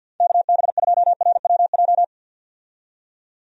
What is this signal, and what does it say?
Morse code, heard as a keyed tone, about 48 wpm, 700 Hz. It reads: KB2RWJ
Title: USA Call Signs - 6 Characters - 50wpm